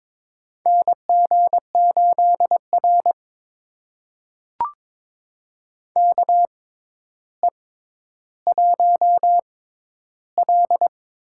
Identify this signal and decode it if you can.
Morse code, heard as a keyed tone, about 22 wpm, spaced as for 11 wpm, 700 Hz. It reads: NG8R KE1L
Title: USA Call Signs - 6x Character Spacing - 22wpm